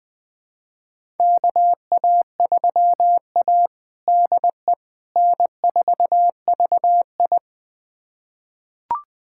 Morse code, heard as a keyed tone, about 21 wpm, 700 Hz. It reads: KA3A DE N4VI
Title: USA Callsign DE Callsign - 4 Characters - 20wpm